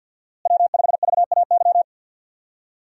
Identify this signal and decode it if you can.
Morse code, heard as a keyed tone, about 51 wpm, 700 Hz. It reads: W5VAY
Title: USA Call Signs - 5 Characters - Repeated 2x - 50wpm